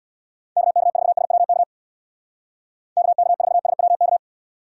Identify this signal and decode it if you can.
Morse code, heard as a keyed tone, about 61 wpm, 700 Hz. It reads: KK4SWK KK4SWK
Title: Speed Racing - USA 2x3 Call Signs 50wpm